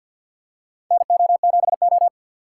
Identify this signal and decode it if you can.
Morse code, heard as a keyed tone, about 52 wpm, 700 Hz. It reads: NO7O
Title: USA Call Signs - 4 Characters - 6x Character Spacing - 50wpm